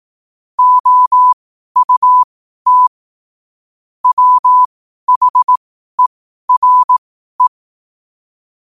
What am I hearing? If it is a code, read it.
Morse code, heard as a keyed tone, about 18 wpm, spaced as for 12 wpm, 1000 Hz. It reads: OUT WHERE
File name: qotdfiles / 12wpm